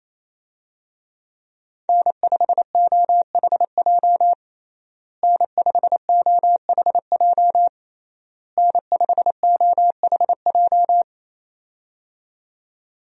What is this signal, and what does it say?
Morse code, heard as a keyed tone, about 28 wpm, 700 Hz. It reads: N5OHJ N5OHJ N5OHJ
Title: USA Call Signs - 5 Characters - Repeated 3x - 28wpm